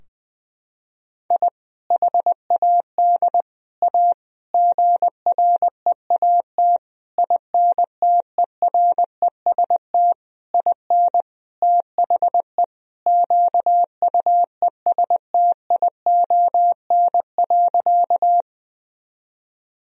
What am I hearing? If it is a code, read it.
Morse code, heard as a keyed tone, about 20 wpm, 700 Hz. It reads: I HAD A GREAT INTEREST IN THE QUESTION.
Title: Sentences from Top 1000 Words - Review Part 2 - 20wpm